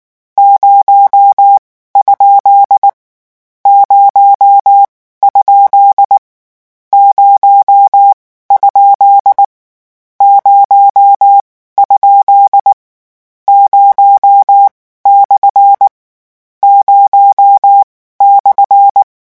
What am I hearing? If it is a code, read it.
Morse code, heard as a keyed tone, about 19 wpm, spaced as for 14 wpm, 800 Hz. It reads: 0 ? 0 ? 0? 0? 0 / 0/
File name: morse mp3 / lecon13-vitesse 18